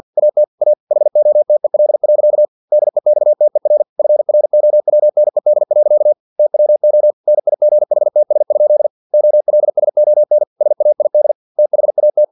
Morse code, heard as a keyed tone, about 50 wpm, 600 Hz. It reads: AT A HOTEL. DEXTER FROWNED. TWO NIGHTS? OLSON SAID THAT
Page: Never 49wpm